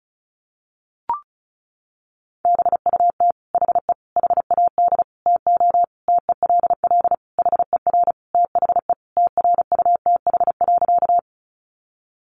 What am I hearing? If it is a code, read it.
Morse code, heard as a keyed tone, about 34 wpm, 700 Hz. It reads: BUT HE HAD TO TELL HER THE TRUTH.
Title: Sentences from Top 800 Words 35wpm